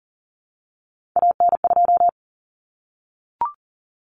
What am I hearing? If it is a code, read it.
Morse code, heard as a keyed tone, about 40 wpm, 700 Hz. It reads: AN2
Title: Common Cut Numbers - 3-Digit 40wpm